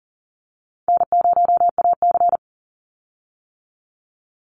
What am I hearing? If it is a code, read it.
Morse code, heard as a keyed tone, about 40 wpm, 700 Hz. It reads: N0AC
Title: Easy Call Signs - 40wpm